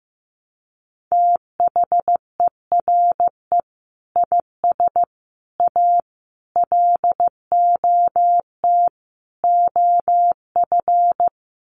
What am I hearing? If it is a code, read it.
Morse code, heard as a keyed tone, about 15 wpm, 700 Hz. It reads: THERE IS A LOT OF
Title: Sets of 5 Words 15wpm